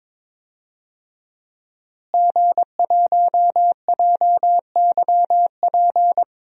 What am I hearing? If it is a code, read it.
Morse code, heard as a keyed tone, about 22 wpm, 700 Hz. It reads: G1JYP
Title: DXCC-100 Call Signs - 5 Characters - Repeated 3x - 22wpm